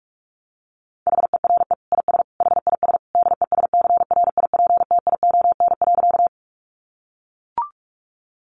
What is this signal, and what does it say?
Morse code, heard as a keyed tone, about 44 wpm, 700 Hz. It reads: HERE IS HIS DESCRIPTION.
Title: Sentences from Top 1300 Words 45wpm